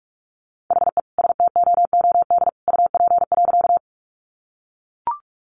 Morse code, heard as a keyed tone, about 44 wpm, 700 Hz. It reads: HE STOOD UP.
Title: Sentences from Top 500 Words 45wpm